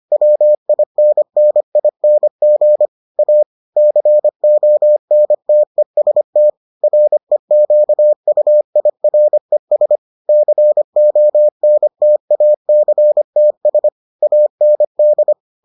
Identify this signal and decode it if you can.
Morse code, heard as a keyed tone, about 25 wpm, 600 Hz. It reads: WINNING A CONTEST REQUIRES CONTACTS AND